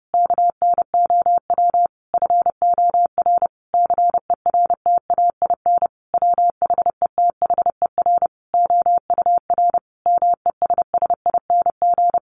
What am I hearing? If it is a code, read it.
Morse code, heard as a keyed tone, about 30 wpm, 700 Hz. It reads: KNOW FOR CERTAIN WHETHER OUR MESSING
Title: A Sound Of Thunder 30wpm